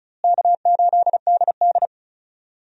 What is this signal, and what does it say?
Morse code, heard as a keyed tone, about 36 wpm, 700 Hz. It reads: K8DD
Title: USA Call Signs - 4 Characters - Repeated 3x - 35wpm